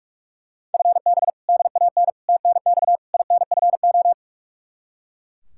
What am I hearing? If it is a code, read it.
Morse code, heard as a keyed tone, about 41 wpm, 700 Hz. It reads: FB DAN TNX INFO
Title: Practice QSO - Exchange 3 - More Info - 45wpm